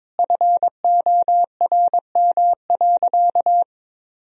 Morse code, heard as a keyed tone, about 21 wpm, 700 Hz. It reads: FORM.
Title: Sentences from Top 1000 Words - Review Part 1 - 22wpm